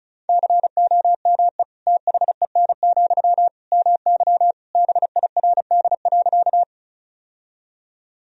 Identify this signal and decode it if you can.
Morse code, heard as a keyed tone, about 34 wpm, 700 Hz. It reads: COME THEN, MY BIRD.
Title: Sentences from Top 1000 Words - Review Part 2 - 35wpm